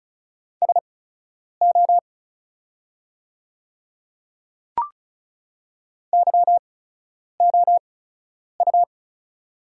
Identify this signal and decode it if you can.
Morse code, heard as a keyed tone, about 36 wpm, spaced as for 10 wpm, 700 Hz. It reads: SO YOU
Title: 8x Character Spacing - Top 100 Words - 35wpm